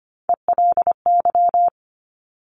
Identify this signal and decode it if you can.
Morse code, heard as a keyed tone, about 25 wpm, 700 Hz. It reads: ELY
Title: Top Three Letter Combinations in Words 25wpm